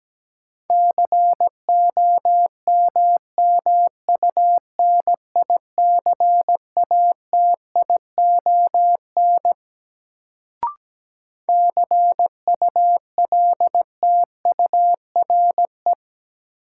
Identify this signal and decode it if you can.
Morse code, heard as a keyed tone, about 17 wpm, 700 Hz. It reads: COMMUNICATION CULTURE
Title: Sentences from Top 1000 Words - Review Part 6 - New Word Drill - 17wpm